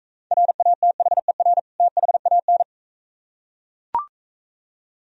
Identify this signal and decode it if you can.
Morse code, heard as a keyed tone, about 42 wpm, 690 Hz. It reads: RATHER THAN